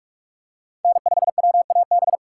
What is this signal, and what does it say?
Morse code, heard as a keyed tone, about 44 wpm, 700 Hz. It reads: N5WAB
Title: USA Call Signs - 5 Characters - 6x Character Spacing - 45wpm